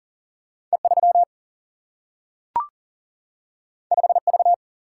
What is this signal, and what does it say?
Morse code, heard as a keyed tone, about 39 wpm, 700 Hz. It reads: E3 5V
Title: Single-Letter Single-Number 40wpm